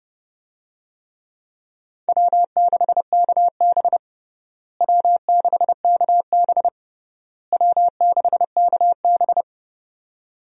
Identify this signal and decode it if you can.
Morse code, heard as a keyed tone, about 32 wpm, 700 Hz. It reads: W6KB W6KB W6KB
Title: USA Call Signs - 4 Characters - Repeated 3x - 30wpm